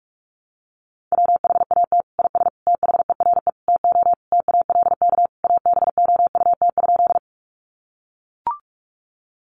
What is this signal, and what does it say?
Morse code, heard as a keyed tone, about 44 wpm, 700 Hz. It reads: WHAT IS THERE TO TALK ABOUT?
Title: Sentences from Top 400 Words 45wpm